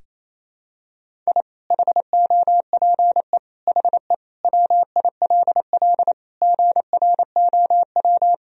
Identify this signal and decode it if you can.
Morse code, heard as a keyed tone, about 28 wpm, 700 Hz. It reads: I HOPE HE WILL GROW
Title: Sentences from Top 1000 Words - Review Part 1 - 28wpm